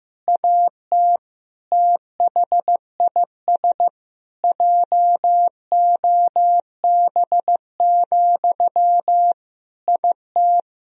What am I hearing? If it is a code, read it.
Morse code, heard as a keyed tone, about 15 wpm, 700 Hz. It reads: AT THIS JOB, IT
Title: Sentences from Top 1000 Words - Review Part 4 - 15wpm